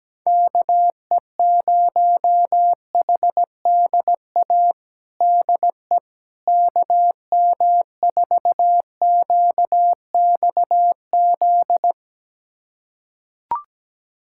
Morse code, heard as a keyed tone, about 17 wpm, 700 Hz. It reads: KE0HDA DE KM4QXZ
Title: USA Callsign DE Callsign - 6 Characters - 17wpm